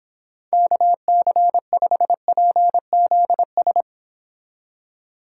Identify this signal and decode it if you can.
Morse code, heard as a keyed tone, about 26 wpm, 700 Hz. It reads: KC5PZS